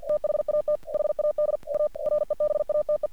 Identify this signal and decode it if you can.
Morse code, heard as a keyed tone, about 48 wpm, 600 Hz. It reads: THAT HAD A REBATE
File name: archived / CRSnet